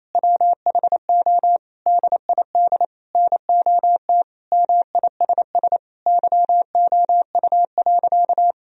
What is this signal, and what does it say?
Morse code, heard as a keyed tone, about 27 wpm, 700 Hz. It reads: WHO DID NOT MISS YOU.
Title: Sentences from Top 1000 Words - Review Part 3 - 28wpm